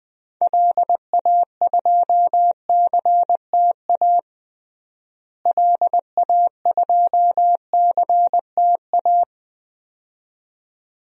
Morse code, heard as a keyed tone, about 20 wpm, 700 Hz. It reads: LA2CTA LA2CTA
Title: DXCC-100 Call Signs - 6 Characters - Repeated 2x - 20wpm